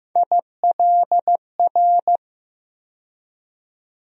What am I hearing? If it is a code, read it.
Morse code, heard as a keyed tone, about 15 wpm, 700 Hz. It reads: ILR
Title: USA Call Sign Postfixes - 3 Letters 15wpm